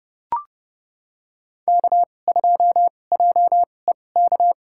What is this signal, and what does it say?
Morse code, heard as a keyed tone, about 30 wpm, spaced as for 22 wpm, 700 Hz. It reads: K2JEK
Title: USA Call Signs - 5 Characters - 2x Character Spacing - 30wpm